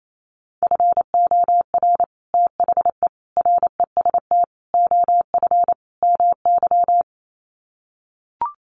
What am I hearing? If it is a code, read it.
Morse code, heard as a keyed tone, about 28 wpm, 700 Hz. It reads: FOR THE REST OF MY
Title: Sets of 5 Words 28wpm